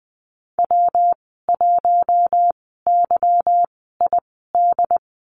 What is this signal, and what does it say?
Morse code, heard as a keyed tone, about 19 wpm, spaced as for 14 wpm, 700 Hz. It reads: W1YID
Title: 179 Lesson 29g - Slash - Call Signs 20wpm